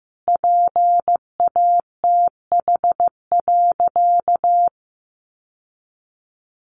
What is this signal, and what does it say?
Morse code, heard as a keyed tone, about 15 wpm, 700 Hz. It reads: PATH.
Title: Sentences from Top 1400 Words 15wpm